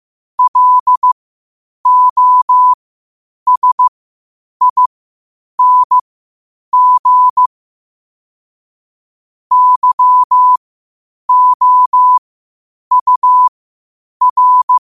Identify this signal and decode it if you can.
Morse code, heard as a keyed tone, about 15 wpm, spaced as for 8 wpm, 1000 Hz. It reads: LOSING YOUR
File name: qotdfiles / 8wpm